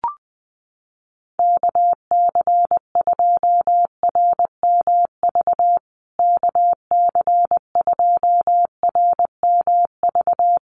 Morse code, heard as a keyed tone, about 20 wpm, 700 Hz. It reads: KC2RMV KC2RMV
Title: USA Call Signs Repeated 3x 20wpm